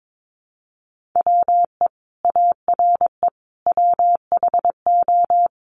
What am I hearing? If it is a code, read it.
Morse code, heard as a keyed tone, about 22 wpm, 700 Hz. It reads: WE ARE WHO
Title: Sentences from Top 200 Words 22wpm